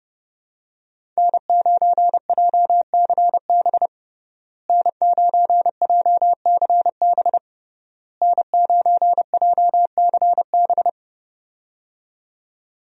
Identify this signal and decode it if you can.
Morse code, heard as a keyed tone, about 30 wpm, 700 Hz. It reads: N9JCB N9JCB N9JCB
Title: USA Call Signs - 5 Characters - Repeated 3x - 30wpm